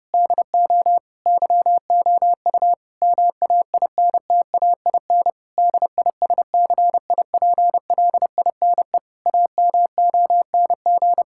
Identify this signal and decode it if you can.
Morse code, heard as a keyed tone, about 30 wpm, 700 Hz. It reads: DO YOU MAINTAIN DISCIPLINE AMONG
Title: Sentences from Top 1000 Words - Review Part 6 - 30wpm